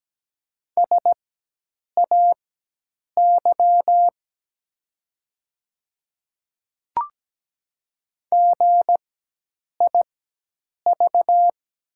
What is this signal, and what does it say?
Morse code, heard as a keyed tone, about 17 wpm, spaced as for 8 wpm, 700 Hz. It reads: SAY GIV
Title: 4x Character Spacing - Top 200 Words - 17wpm